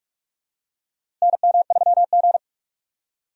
Morse code, heard as a keyed tone, about 46 wpm, 700 Hz. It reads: NM3G
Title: USA Call Signs - 4 Characters - 45wpm